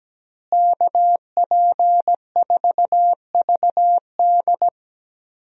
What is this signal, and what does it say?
Morse code, heard as a keyed tone, about 17 wpm, 700 Hz. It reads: KP4VD
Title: DXCC-100 Call Signs - 5 Characters - Repeated 3x - 17wpm